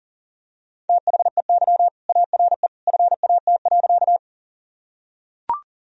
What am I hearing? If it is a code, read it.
Morse code, heard as a keyed tone, about 40 wpm, 700 Hz. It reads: THEY ARE FAT.
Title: Sentences from Top 1000 Words - Review Part 4 - 40wpm